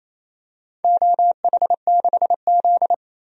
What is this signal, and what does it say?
Morse code, heard as a keyed tone, about 28 wpm, 700 Hz. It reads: OH6Z
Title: DXCC-100 Call Signs - 4 Characters - 28wpm